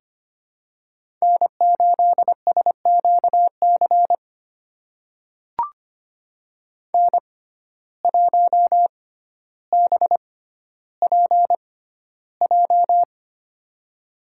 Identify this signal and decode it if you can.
Morse code, heard as a keyed tone, about 25 wpm, spaced as for 13 wpm, 700 Hz. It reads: N8SQC N1BPJ
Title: USA Call Signs - 5 Characters - 6x Character Spacing - 25wpm